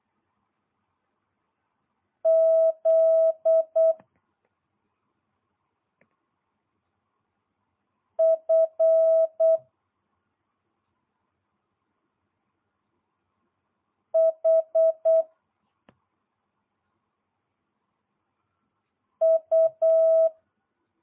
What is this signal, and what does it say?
Morse code, heard as a keyed tone, about 8 wpm, 650 Hz. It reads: Z F H U